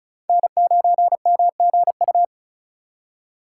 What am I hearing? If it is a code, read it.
Morse code, heard as a keyed tone, about 47 wpm, 700 Hz. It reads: N9MGU